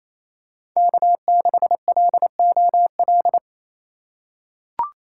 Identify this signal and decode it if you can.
Morse code, heard as a keyed tone, about 28 wpm, 700 Hz. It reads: K6LOL
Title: USA Call Signs - 4x Character Spacing - 28wpm